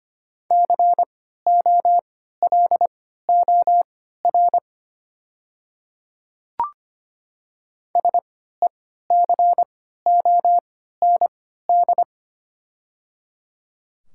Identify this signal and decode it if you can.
Morse code, heard as a keyed tone, about 25 wpm, spaced as for 14 wpm, 700 Hz. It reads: COLOR SECOND
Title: 3x Character Spacing - Top 300 Words - 25wpm